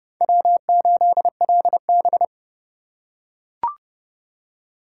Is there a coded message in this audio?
Yes: Morse code from 0.2 to 2.2 s, heard as a keyed tone, about 29 wpm, 700 Hz. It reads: W8LB